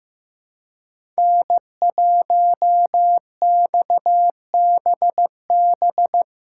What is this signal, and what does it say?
Morse code, heard as a keyed tone, about 15 wpm, 700 Hz. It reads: N1XBB
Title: USA Call Signs - 5 Characters - 3x Character Spacing - 15wpm